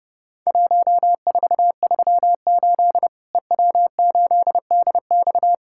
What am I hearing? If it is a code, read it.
Morse code, heard as a keyed tone, about 32 wpm, 700 Hz. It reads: 1438 EW8DX
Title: Worked All Europe DX Contest - QTC Exchange - 30wpm